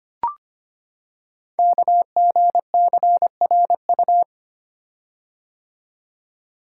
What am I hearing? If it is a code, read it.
Morse code, heard as a keyed tone, about 25 wpm, 700 Hz. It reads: KGCRU
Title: Code Groups - 5 Letters - 25wpm